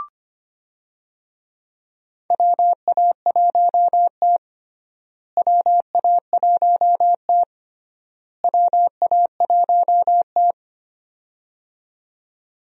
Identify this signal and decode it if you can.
Morse code, heard as a keyed tone, about 25 wpm, 700 Hz. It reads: WA1T WA1T WA1T
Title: USA Call Signs - 4 Characters - Repeated 3x - 25wpm